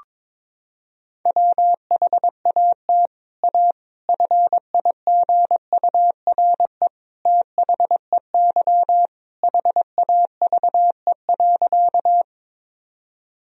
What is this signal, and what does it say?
Morse code, heard as a keyed tone, about 22 wpm, 700 Hz. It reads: WHAT A FIGURE THEY HAVE.
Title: Sentences from Top 1000 Words - Review Part 2 - 22wpm